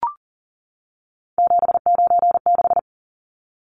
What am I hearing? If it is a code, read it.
Morse code, heard as a keyed tone, about 40 wpm, 700 Hz. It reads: 796
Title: Numbers - 3-Digit 40wpm